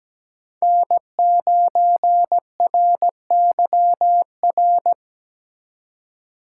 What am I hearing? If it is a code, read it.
Morse code, heard as a keyed tone, about 17 wpm, 700 Hz. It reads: N9RYR
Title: USA Call Signs - 3x Character Spacing - 17wpm